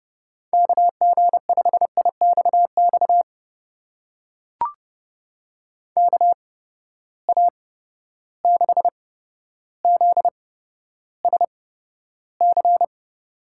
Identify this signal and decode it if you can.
Morse code, heard as a keyed tone, about 31 wpm, spaced as for 13 wpm, 700 Hz. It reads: KG5IXX KA6ZSC
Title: USA Call Signs - 6 Characters - 8x Character Spacing - 30wpm